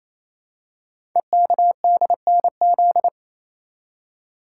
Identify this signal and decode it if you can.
Morse code, heard as a keyed tone, about 27 wpm, 700 Hz. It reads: EKDNZ
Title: Code Groups - 5 Letters - 28wpm